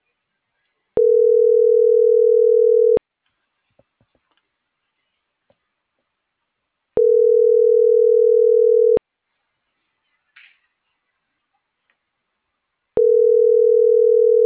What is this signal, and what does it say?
Ringback tone (North America), 440 + 480 Hz.